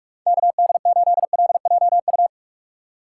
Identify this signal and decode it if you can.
Morse code, heard as a keyed tone, about 46 wpm, 700 Hz. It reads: KD8LJU
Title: USA Call Signs - 6 Characters - 6x Character Spacing - 45wpm